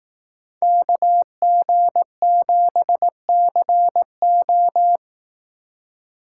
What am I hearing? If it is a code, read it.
Morse code, heard as a keyed tone, about 18 wpm, 700 Hz. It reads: KG7CO
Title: Speed Racing - USA 2x2 Call Signs 15wpm